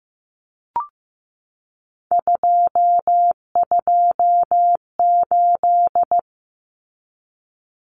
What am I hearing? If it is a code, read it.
Morse code, heard as a keyed tone, about 15 wpm, 700 Hz. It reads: 228
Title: Numbers - 3-Digit 15wpm